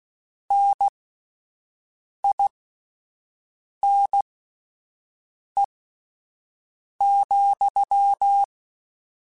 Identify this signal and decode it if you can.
Morse code, heard as a keyed tone, about 16 wpm, spaced as for 5 wpm, 800 Hz. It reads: NINE,
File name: Misc / Misc-mp3s / Word Pairs @5